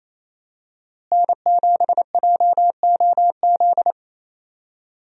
Morse code, heard as a keyed tone, about 28 wpm, 700 Hz. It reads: N7JOZ
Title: USA Call Signs - 5 Characters - Repeated 2x - 28wpm